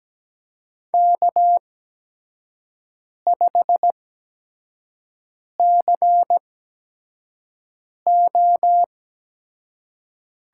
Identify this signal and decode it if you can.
Morse code, heard as a keyed tone, about 17 wpm, spaced as for 5 wpm, 700 Hz. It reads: K5CO
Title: USA Call Signs - 5 Characters - 8x Character Spacing - 17wpm